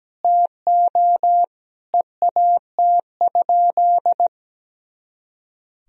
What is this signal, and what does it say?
Morse code, heard as a keyed tone, about 17 wpm, 700 Hz. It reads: TO EAT?
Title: Sentences from Top 300 Words and Abbreviations 17wpm